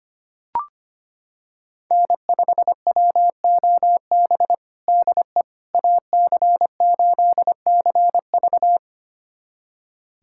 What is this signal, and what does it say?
Morse code, heard as a keyed tone, about 25 wpm, 700 Hz. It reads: N5WOB DE AC8CV
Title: USA Callsign DE Callsign - 5 Characters - 25wpm